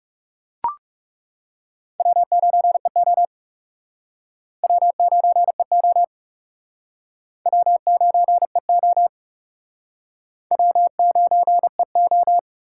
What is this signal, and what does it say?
Morse code, heard as a keyed tone, about 37 wpm, 700 Hz. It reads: W9EO W9EO W9EO W9EO
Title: Speed Racing - USA 1x2 Call Signs 30wpm